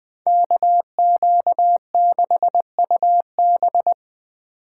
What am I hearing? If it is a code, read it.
Morse code, heard as a keyed tone, about 20 wpm, 700 Hz. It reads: KQ6UB
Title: Speed Racing - USA 2x2 Call Signs 17wpm